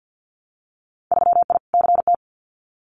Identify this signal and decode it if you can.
Morse code, heard as a keyed tone, about 50 wpm, 700 Hz. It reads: 3I CT
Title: Winter Field Day Exchanges 50wpm